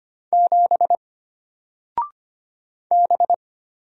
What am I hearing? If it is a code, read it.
Morse code, heard as a keyed tone, about 25 wpm, 700 Hz. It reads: 7 B
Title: Single Character (Letter:Number:Punctuation:Prosign) Alt 25wpm